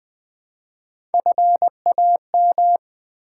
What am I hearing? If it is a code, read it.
Morse code, heard as a keyed tone, about 20 wpm, 700 Hz. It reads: FAM
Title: Build A Word - 1 Letter At A Time - 4-Letter Words - 20wpm